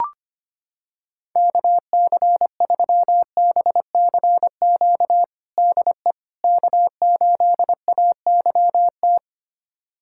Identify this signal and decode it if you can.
Morse code, heard as a keyed tone, about 25 wpm, 700 Hz. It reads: KC3BCQ DE K8AYT
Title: USA Callsign DE Callsign 25wpm